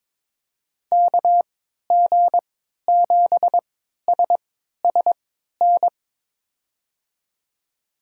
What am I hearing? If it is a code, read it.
Morse code, heard as a keyed tone, about 22 wpm, spaced as for 12 wpm, 700 Hz. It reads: KG7SSN
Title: USA Call Signs - 6 Characters - 3x Character Spacing - 22wpm